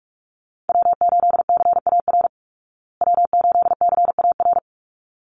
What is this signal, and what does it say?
Morse code, heard as a keyed tone, about 44 wpm, 700 Hz. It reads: W8CAR W8CAR
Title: Call Signs Repeated 3x - 45wpm